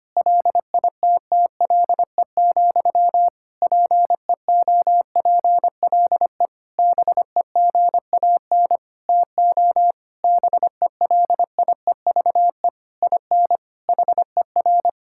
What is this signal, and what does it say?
Morse code, heard as a keyed tone, about 25 wpm, 700 Hz. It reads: LITTLE, PEOPLE BEGAN TO BELIEVE IN HER
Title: Sentences from Top 1600 Words 25wpm